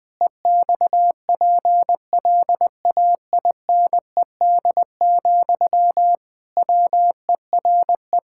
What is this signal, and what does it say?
Morse code, heard as a keyed tone, about 19 wpm, 700 Hz. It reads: EXPLAINED, WERE
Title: Pangram Challenge v2 - 20wpm